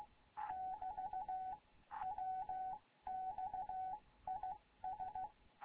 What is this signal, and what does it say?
Morse code, heard as a keyed tone, about 16 wpm, spaced as for 13 wpm, 750 Hz. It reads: <BT> WX IS
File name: Morze-HA2MN / hangp / 2 11 GYAKORLAT